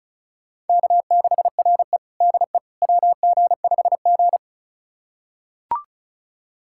Morse code, heard as a keyed tone, about 36 wpm, 700 Hz. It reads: K6RE DE WG5G
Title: USA Callsign DE Callsign - 4 Characters - 35wpm